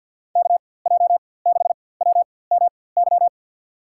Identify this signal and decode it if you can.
Morse code, heard as a keyed tone, about 53 wpm, spaced as for 24 wpm, 700 Hz. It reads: KJ6WMY
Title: USA Call Signs - 6 Characters - 4x Character Spacing - 50wpm